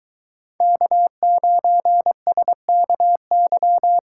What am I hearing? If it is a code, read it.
Morse code, heard as a keyed tone, about 23 wpm, 700 Hz. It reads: K9SKY